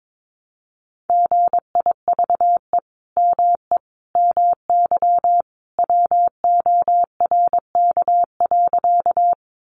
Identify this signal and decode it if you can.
Morse code, heard as a keyed tone, about 22 wpm, 700 Hz. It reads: GIVE ME MY WORK.
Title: Sentences from Top 200 Words 22wpm